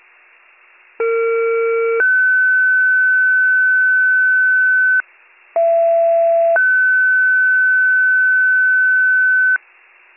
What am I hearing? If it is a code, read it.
Two-tone page, heard as a one-second tone, then a three-second tone, about 460 then 1500 Hz, about 670 then 1500 Hz. It is heard from 1.0 to 9.6 s.